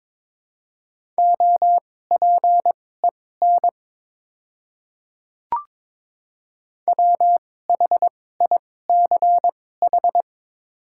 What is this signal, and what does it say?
Morse code, heard as a keyed tone, about 22 wpm, spaced as for 16 wpm, 700 Hz. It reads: OPEN WHICH
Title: 2x Character Spacing - Top 300 Words - 22wpm